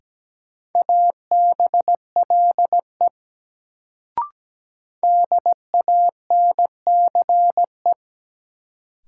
Morse code, heard as a keyed tone, about 17 wpm, 700 Hz. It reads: ABLE DANCE
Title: Sentences from Top 700 Words - New Word Drill - 17wpm